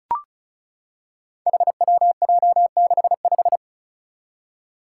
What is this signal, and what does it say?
Morse code, heard as a keyed tone, about 36 wpm, 700 Hz. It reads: HWJ65
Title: Five Challenging Letter-Number Combinations 35wpm